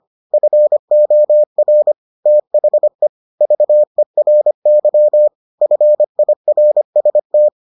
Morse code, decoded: FOR THE VERY FIRST

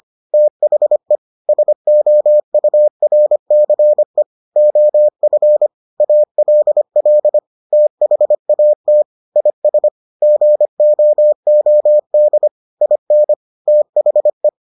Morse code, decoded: THE SOURCE OF ALL THAT IS GOOD IN THE

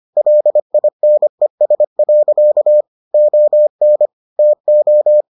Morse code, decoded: LINES. ON TO

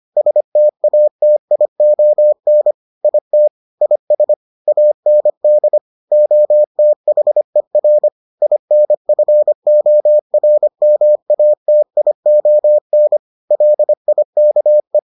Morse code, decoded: STATION IT IS AND OTHER INFORMATION LIKE

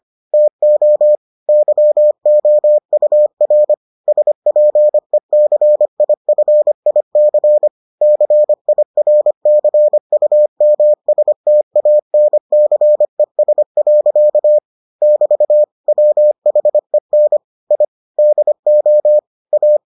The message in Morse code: TO YOUR SPECIFIC CIRCUMSTANCES. <BT> WHEN I DO A

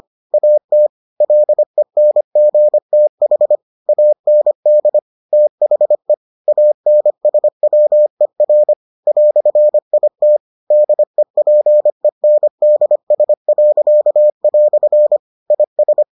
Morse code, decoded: AT LENGTH AND THE ANSWER "IT DEPENDS." IS

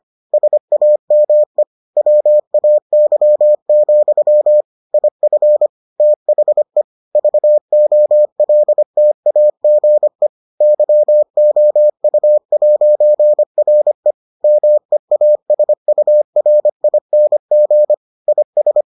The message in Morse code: SAME WAY, IF THE VOLTAGE YOU'RE MEASURING IS

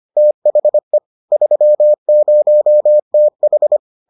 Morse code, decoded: THE 30TH

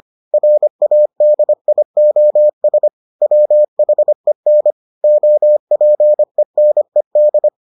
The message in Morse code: RADIOS WHEN OPENED

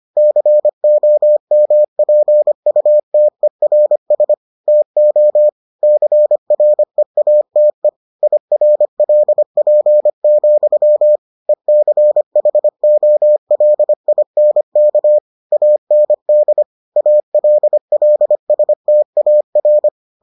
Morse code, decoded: COMPUTERS TO CREATE IRLP, ECHOLINK AND ALLSTAR

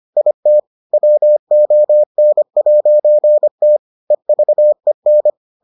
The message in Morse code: IT WON'T EVEN

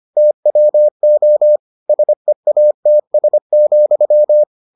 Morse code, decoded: TWO SEATS,

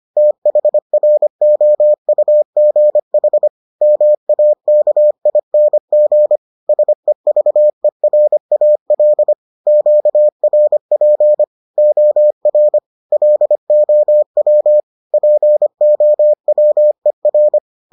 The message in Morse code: THROUGH MAKING SEVERAL QRP OR LOW POWER

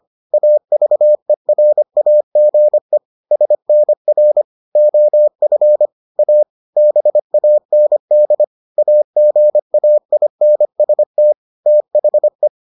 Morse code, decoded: AVERAGE SNR OF A BAND AGAINST THE

